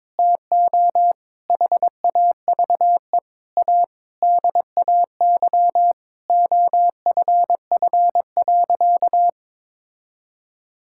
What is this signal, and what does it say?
Morse code, heard as a keyed tone, about 22 wpm, 700 Hz. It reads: TO HAVE A DAY OFF.
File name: Sentences with 7-Word Phrases - 22wpm